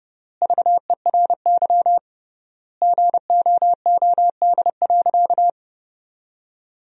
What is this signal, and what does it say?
Morse code, heard as a keyed tone, about 29 wpm, 700 Hz. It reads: VERY GOOD.
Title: 3x Word-Spacing - Sentences from Top 200 Words - 30wpm